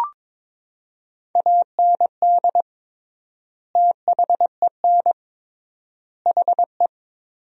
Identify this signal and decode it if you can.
Morse code, heard as a keyed tone, about 21 wpm, 700 Hz. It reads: AND THEN HE
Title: 3x Word-Spacing - Sentences from Top 400 Words - 22wpm